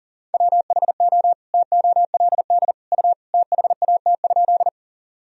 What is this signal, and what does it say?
Morse code, heard as a keyed tone, about 40 wpm, 700 Hz. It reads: WHO TOLD U THAT?
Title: Sentences from Top 400 Words and Abbreviations 40wpm